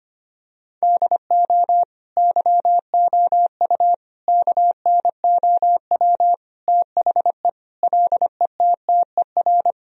Morse code, heard as a keyed tone, about 25 wpm, 700 Hz. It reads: DO YOU KNOW THE LETTER
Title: Sentences from Top 1000 Words - Review Part 2 - 25wpm